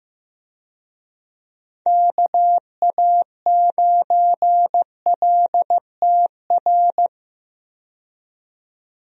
Morse code, heard as a keyed tone, about 15 wpm, 700 Hz. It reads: KA9LTR
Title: USA Call Signs - 6 Characters - 15wpm